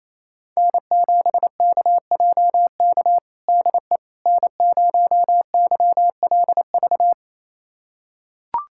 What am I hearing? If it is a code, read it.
Morse code, heard as a keyed tone, about 28 wpm, 700 Hz. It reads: N7KJK DE N0YLV
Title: USA Callsign DE Callsign - 5 Characters - 28wpm